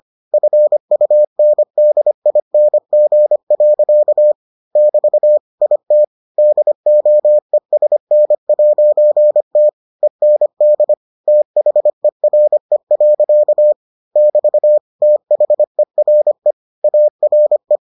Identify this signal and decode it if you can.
Morse code, heard as a keyed tone, about 25 wpm, 600 Hz. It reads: FUNDING. <BT> IT DOESN'T END THERE. <BT> THERE ARE